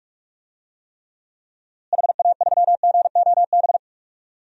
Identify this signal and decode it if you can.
Morse code, heard as a keyed tone, about 46 wpm, 700 Hz. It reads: HA3GOB
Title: DXCC-100 Call Signs - 6 Characters - Repeated 3x - 45wpm